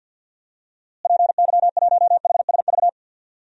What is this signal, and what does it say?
Morse code, heard as a keyed tone, about 47 wpm, 700 Hz. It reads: PY1HSV
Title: DXCC-100 Call Signs - 6 Characters - Repeated 2x - 50wpm